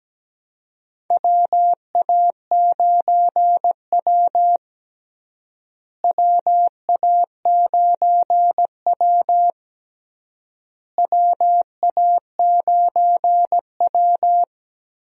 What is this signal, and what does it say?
Morse code, heard as a keyed tone, about 17 wpm, 700 Hz. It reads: WA9W WA9W WA9W
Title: USA Call Signs - 4 Characters - Repeated 3x - 17wpm